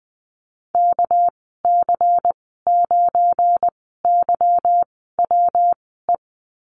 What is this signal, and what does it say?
Morse code, heard as a keyed tone, about 20 wpm, spaced as for 14 wpm, 700 Hz. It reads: KC9YWE
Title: 213 Lesson 34g - Number 8 - Call Signs 20wpm